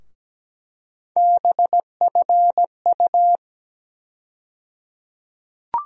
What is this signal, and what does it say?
Morse code, heard as a keyed tone, about 17 wpm, 700 Hz. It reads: BFU
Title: USA Call Sign Postfixes - 3 Letters 17wpm